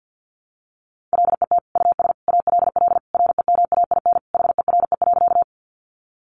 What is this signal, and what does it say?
Morse code, heard as a keyed tone, about 50 wpm, 700 Hz. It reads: LET US ALL REMAIN HERE.
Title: Sentences from Top 1200 Words 50wpm